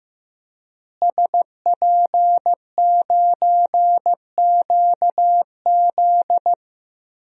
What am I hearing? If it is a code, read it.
Morse code, heard as a keyed tone, about 15 wpm, 700 Hz. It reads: SP9QZ
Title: DXCC-100 Call Signs - 5 Characters - Repeated 3x - 15wpm